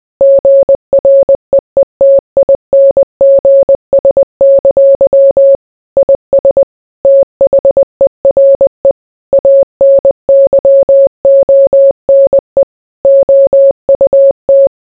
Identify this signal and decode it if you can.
Morse code, heard as a keyed tone, about 20 wpm, 550 Hz. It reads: GREETINGS! IS THERE ANYONE OUT